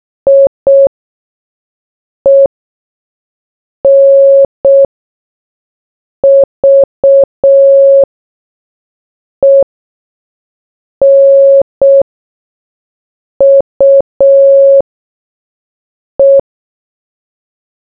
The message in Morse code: IENVENUE